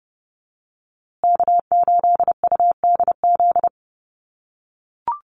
Morse code, heard as a keyed tone, about 30 wpm, 700 Hz. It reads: K8UDZ
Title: USA Call Signs Repeated 3x 30wpm